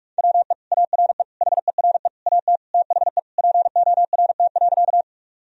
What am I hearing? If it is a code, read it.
Morse code, heard as a keyed tone, about 46 wpm, 700 Hz. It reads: WE ARE HERE AT THE PORT.
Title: Sentences from Top 1000 Words - Review Part 1 - 45wpm